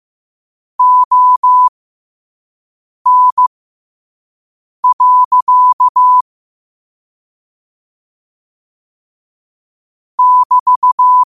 Morse code, heard as a keyed tone, about 15 wpm, spaced as for 5 wpm, 1000 Hz. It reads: ON. <BT>